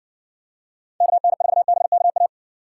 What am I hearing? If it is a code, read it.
Morse code, heard as a keyed tone, about 63 wpm, 700 Hz. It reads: KN4BKA